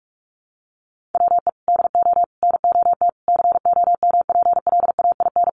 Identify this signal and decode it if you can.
Morse code, heard as a keyed tone, about 48 wpm, 700 Hz. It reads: WE DO NOT COMPLAIN
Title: Sentences from Top 1300 Words 45wpm